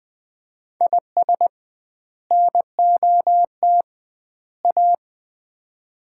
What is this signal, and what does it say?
Morse code, heard as a keyed tone, about 21 wpm, 700 Hz. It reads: IS NOT A
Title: 2x Word-Spacing - Sets of 3 Words - 20wpm